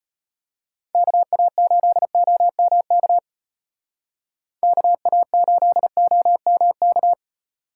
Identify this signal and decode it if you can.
Morse code, heard as a keyed tone, about 31 wpm, 700 Hz. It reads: KA8OMK KA8OMK
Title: Speed Racing - USA 2x3 Call Signs 25wpm